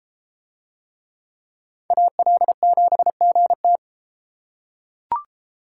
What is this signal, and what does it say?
Morse code, heard as a keyed tone, about 30 wpm, 700 Hz. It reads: AL7GT